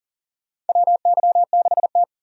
Morse code, heard as a keyed tone, about 42 wpm, 700 Hz. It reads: WY6T
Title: USA Call Signs - 4 Characters - 4x Character Spacing - 40wpm